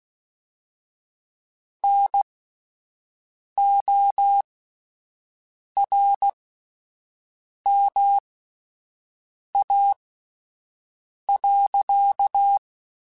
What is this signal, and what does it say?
Morse code, heard as a keyed tone, about 16 wpm, spaced as for 5 wpm, 800 Hz. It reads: NORMA.
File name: QSOs / QSOs-mp3s / WD0FBT 1@10